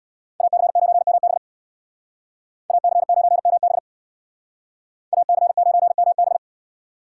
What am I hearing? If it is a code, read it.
Morse code, heard as a keyed tone, about 64 wpm, 700 Hz. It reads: AC9MB AC9MB AC9MB